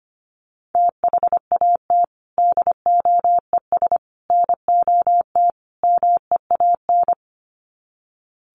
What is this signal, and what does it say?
Morse code, heard as a keyed tone, about 25 wpm, 700 Hz. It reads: THAT DOES NOT MEAN
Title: Sets of 4 Words 25wpm